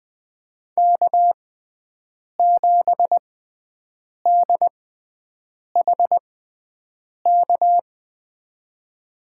Morse code, heard as a keyed tone, about 20 wpm, spaced as for 7 wpm, 700 Hz. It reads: K7DHK
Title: USA Call Signs - 5 Characters - 6x Character Spacing - 20wpm